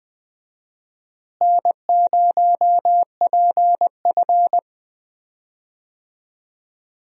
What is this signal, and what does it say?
Morse code, heard as a keyed tone, about 21 wpm, 700 Hz. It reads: N0PF
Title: USA Call Signs - 4 Characters - Repeated 2x - 20wpm